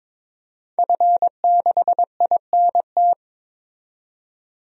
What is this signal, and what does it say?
Morse code, heard as a keyed tone, about 22 wpm, 700 Hz. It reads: F6INT
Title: DXCC-100 Call Signs - 5 Characters - 22wpm